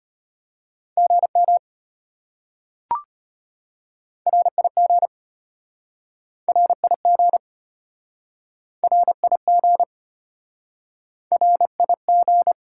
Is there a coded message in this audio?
Yes: Morse code from 1.0 to 12.5 s, heard as a keyed tone, about 34 wpm, 700 Hz. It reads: GM RIG RIG RIG RIG